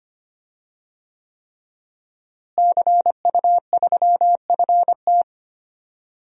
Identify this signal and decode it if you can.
Morse code, heard as a keyed tone, about 25 wpm, 700 Hz. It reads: CU3FT
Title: DXCC-100 Call Signs - 5 Characters - Repeated 3x - 25wpm